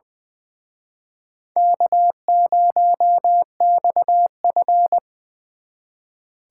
Morse code, heard as a keyed tone, about 23 wpm, 700 Hz. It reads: K0XF
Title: Speed Racing - USA 1x2 Call Signs 20wpm